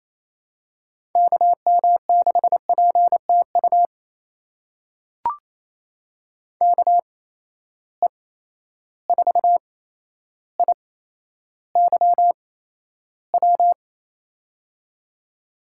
Morse code, heard as a keyed tone, about 28 wpm, spaced as for 12 wpm, 700 Hz. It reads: KM6PTU KE4IYW
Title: USA Call Signs - 6 Characters - 8x Character Spacing - 28wpm